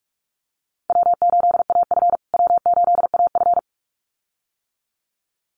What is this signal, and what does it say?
Morse code, heard as a keyed tone, about 44 wpm, 690 Hz. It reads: W8AF W8AF
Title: USA Call Signs Repeated 2x 45wpm